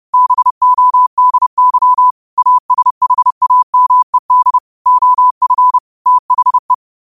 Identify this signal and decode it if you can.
Morse code, heard as a keyed tone, about 30 wpm, 1000 Hz. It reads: BODY ASHAMED OF THE